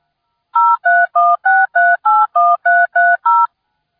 DTMF, heard as telephone touch tones, tones of about 220 ms. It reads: 0316381330